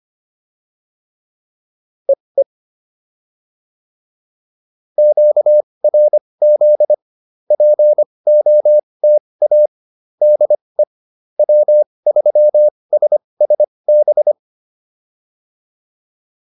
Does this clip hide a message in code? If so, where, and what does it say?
Yes: Morse code from 2.1 to 14.3 s, heard as a keyed tone, about 25 wpm, spaced as for 20 wpm, 600 Hz. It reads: EE QRZ POTA DE W3SSB